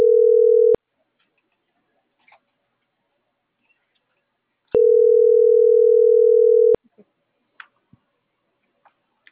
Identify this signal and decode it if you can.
Ringback tone (North America), 440 + 480 Hz.